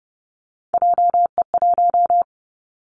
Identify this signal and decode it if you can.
Morse code, heard as a keyed tone, about 30 wpm, 700 Hz. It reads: JE1
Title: Callsign Prefixes for Top 25 DXCC - Repeated 3x - 30wpm